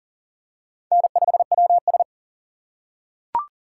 Morse code, heard as a keyed tone, about 41 wpm, 700 Hz. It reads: N5WS
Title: USA Call Signs - 4 Characters - 2x Character Spacing - 40wpm